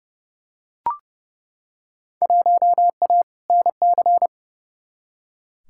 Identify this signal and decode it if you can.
Morse code, heard as a keyed tone, about 29 wpm, 700 Hz. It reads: 1A NC
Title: Field Day Exchanges Encore - 30wpm